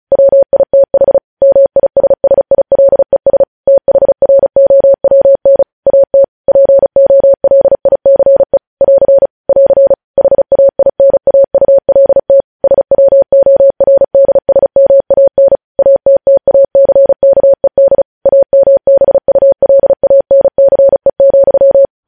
Morse code, heard as a keyed tone, about 36 wpm, 550 Hz. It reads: WITH MISSILES THROWN AT POLICE <AR> <AR> HAINAULT SWORDSMAN ATTACKED AMBULANCE,
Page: news headlines in morse code at 35 wpm